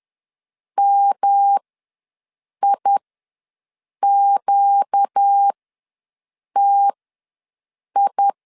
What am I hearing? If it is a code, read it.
Morse code, heard as a keyed tone, about 10 wpm, spaced as for 6 wpm, 800 Hz. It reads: MIQTI
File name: LEZIONI / 5 LBVC / 5 L ws 10wpm 3636